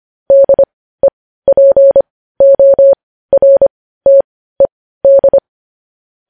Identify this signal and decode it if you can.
Morse code, heard as a keyed tone, about 25 wpm, spaced as for 15 wpm, 550 Hz. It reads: DEPORTED
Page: news headlines in morse code at 15 wpm